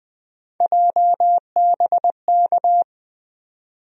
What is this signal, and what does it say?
Morse code, heard as a keyed tone, about 20 wpm, 700 Hz. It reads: JBK
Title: USA Call Sign Postfixes - 3 Letters 20wpm